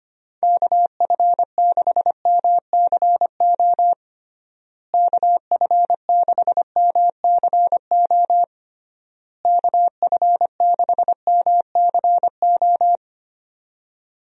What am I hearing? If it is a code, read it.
Morse code, heard as a keyed tone, about 25 wpm, 700 Hz. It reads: KF6MCO KF6MCO KF6MCO
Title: USA Call Signs - 6 Characters - Repeated 3x - 25wpm